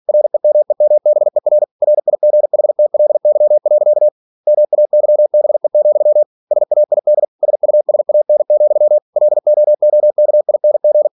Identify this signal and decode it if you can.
Morse code, heard as a keyed tone, about 46 wpm, 600 Hz. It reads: REMEMBER RIGHTLY. MAYBE, SAID SUSAN, LOOKING